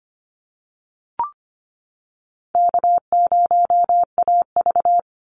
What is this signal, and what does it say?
Morse code, heard as a keyed tone, about 25 wpm, 700 Hz. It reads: K0AV
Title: Easy Call Signs - 25wpm